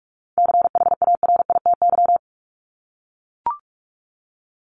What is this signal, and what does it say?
Morse code, heard as a keyed tone, about 44 wpm, 700 Hz. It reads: CHARITY
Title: Top 1501-1750 Words 45wpm